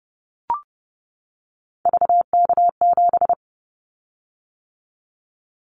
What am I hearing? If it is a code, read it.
Morse code, heard as a keyed tone, about 30 wpm, 700 Hz. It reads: VK7
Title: Callsign Prefixes for Top 100 DXCC - 30wpm